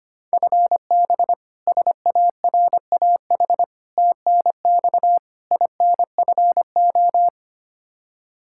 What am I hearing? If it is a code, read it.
Morse code, heard as a keyed tone, about 25 wpm, 700 Hz. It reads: FB SARAH TNX INFO
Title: K1USN SST QSO Practice 25wpm